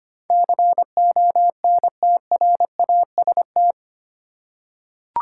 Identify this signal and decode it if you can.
Morse code, heard as a keyed tone, about 25 wpm, 700 Hz. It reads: CONTRAST
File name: Eight Letter Words - 25wpm